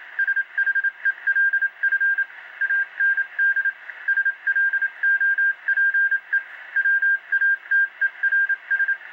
Morse code, heard as a keyed tone, about 27 wpm, 1600 Hz. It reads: SHELF AND SLOPE WATERS